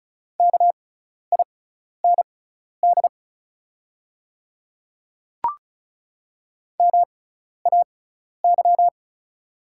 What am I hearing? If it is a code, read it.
Morse code, heard as a keyed tone, about 36 wpm, spaced as for 12 wpm, 700 Hz. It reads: KIND MAY
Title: 6x Character Spacing - Top 200 Words - 35wpm